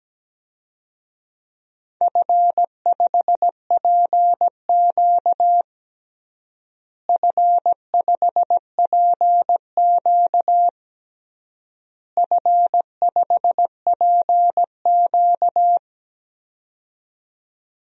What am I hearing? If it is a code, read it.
Morse code, heard as a keyed tone, about 17 wpm, 700 Hz. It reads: F5PQ F5PQ F5PQ
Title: DXCC-100 Call Signs - 4 Characters - Repeated 3x - 17wpm